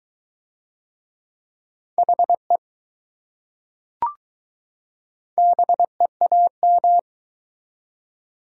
Morse code, heard as a keyed tone, about 20 wpm, 700 Hz. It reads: HE BEAM